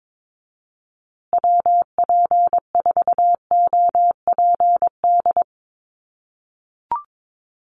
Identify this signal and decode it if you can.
Morse code, heard as a keyed tone, about 22 wpm, 700 Hz. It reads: WP4OPD
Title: USA Call Signs Repeated 2x 22wpm